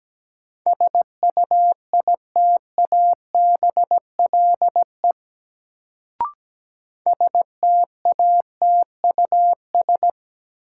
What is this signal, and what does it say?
Morse code, heard as a keyed tone, about 17 wpm, 700 Hz. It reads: SUITABLE STATUS
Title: Sentences from Top 1400 Words - New Word Drill Alt - 17wpm